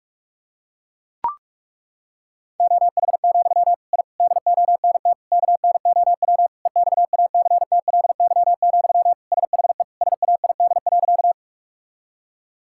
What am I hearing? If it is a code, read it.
Morse code, heard as a keyed tone, about 41 wpm, 700 Hz. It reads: OH, I DONT KNOW EXACTLY, SHE SAID.
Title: Sentences from Top 1000 Words - Review Part 6 - 45wpm